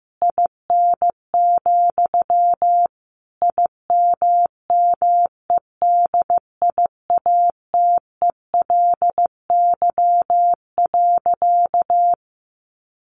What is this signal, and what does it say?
Morse code, heard as a keyed tone, about 15 wpm, 700 Hz. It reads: IN, IMMEDIATELY.
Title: Sentences from Top 1100 Words 15wpm